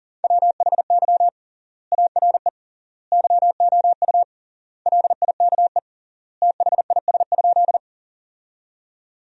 Morse code, heard as a keyed tone, about 40 wpm, 700 Hz. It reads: WHY ARE YOU LIKE THIS?
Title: 3x Word-Spacing - Sentences from Top 200 Words - 40wpm